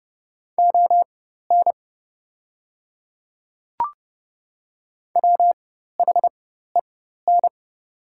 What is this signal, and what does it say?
Morse code, heard as a keyed tone, about 30 wpm, spaced as for 14 wpm, 690 Hz. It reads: ON WHEN
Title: 4x Character Spacing - Top 100 Words - 30wpm